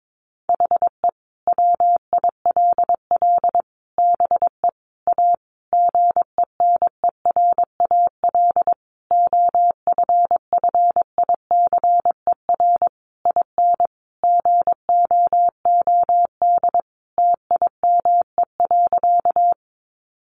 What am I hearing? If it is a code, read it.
Morse code, heard as a keyed tone, about 22 wpm, 700 Hz. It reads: HE WILL BE A GENERAL OFFICER IN GOOD TIME.
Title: Sentences from Top 1200 Words 22wpm